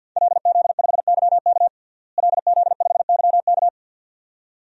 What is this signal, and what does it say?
Morse code, heard as a keyed tone, about 52 wpm, 700 Hz. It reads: LZ5YK LZ5YK
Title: DXCC-100 Call Signs - 5 Characters - Repeated 2x - 50wpm